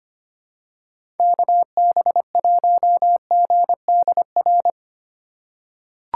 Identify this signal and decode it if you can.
Morse code, heard as a keyed tone, about 25 wpm, 700 Hz. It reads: KB1GDR
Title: USA Call Signs - 6 Characters - 3x Character Spacing - 25wpm